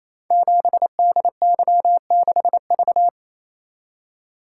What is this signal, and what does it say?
Morse code, heard as a keyed tone, about 28 wpm, 700 Hz. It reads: 7DY6V
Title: Code Groups - 5 Letter:Numbers - 28wpm